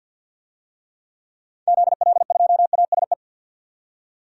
Morse code, heard as a keyed tone, about 45 wpm, 700 Hz. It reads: ZL2ASE
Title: DXCC-100 Call Signs - 6 Characters - 50wpm